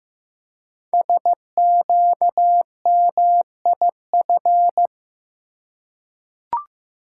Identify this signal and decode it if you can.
Morse code, heard as a keyed tone, about 15 wpm, 700 Hz. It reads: SQMIF
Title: Code Groups - 5 Letters - 15wpm